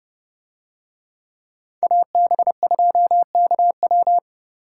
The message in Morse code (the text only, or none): AB2KW